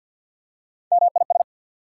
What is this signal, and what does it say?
Morse code, heard as a keyed tone, about 54 wpm, 700 Hz. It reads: MIS
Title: Prefixes and Suffixes 50wpm